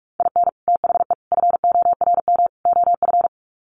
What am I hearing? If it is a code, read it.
Morse code, heard as a keyed tone, about 44 wpm, 700 Hz. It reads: IN THE FORM OF
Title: Sets of 4 Words 45wpm